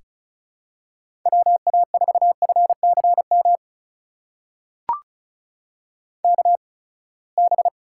Morse code, heard as a keyed tone, about 36 wpm, spaced as for 24 wpm, 700 Hz. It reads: WA4FCM KB
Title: USA Call Signs - 6 Characters - 8x Character Spacing - 35wpm